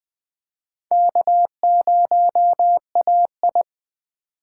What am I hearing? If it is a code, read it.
Morse code, heard as a keyed tone, about 20 wpm, 700 Hz. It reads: K0AI
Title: USA Call Signs - 4 Characters - 4x Character Spacing - 20wpm